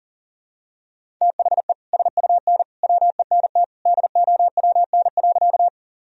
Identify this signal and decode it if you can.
Morse code, heard as a keyed tone, about 40 wpm, 700 Hz. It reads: THE SUN WENT DOWN.
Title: Sentences from Top 1000 Words - Review Part 1 - 40wpm